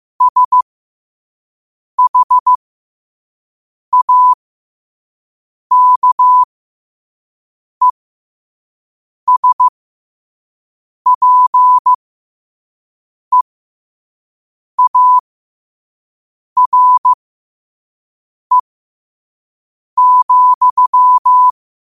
Morse code, heard as a keyed tone, about 15 wpm, spaced as for 5 wpm, 1000 Hz. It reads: SHAKESPEARE,
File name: qotdfiles / 5wpm